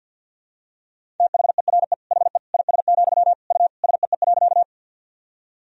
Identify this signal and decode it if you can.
Morse code, heard as a keyed tone, about 49 wpm, 700 Hz. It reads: THERE HE IS, U SEE.
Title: Sentences from Top 100 Words and Abbreviations 50wpm